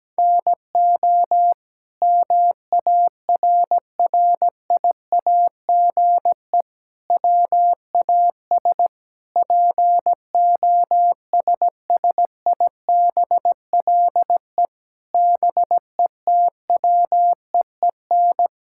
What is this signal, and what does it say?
Morse code, heard as a keyed tone, about 17 wpm, 700 Hz. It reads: NO MARRIAGE WAS POSSIBLE BETWEEN
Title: Sentences from Top 1000 Words - Review Part 5 - 17wpm